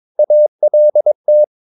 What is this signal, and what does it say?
Morse code, heard as a keyed tone, about 22 wpm, 600 Hz. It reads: ALT
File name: Ferie QRS 22wpm